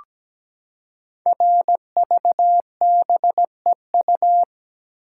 Morse code, heard as a keyed tone, about 17 wpm, 700 Hz. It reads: RVBEU
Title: Code Groups - 5 Letters - 17wpm